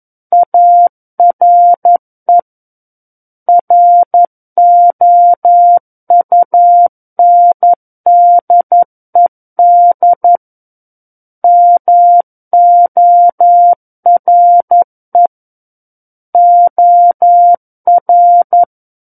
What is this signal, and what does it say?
Morse code, heard as a keyed tone, about 11 wpm, 700 Hz. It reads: ARE ROUNDED MORE OR
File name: Book One / 02-Chapter 2 - 11 WPM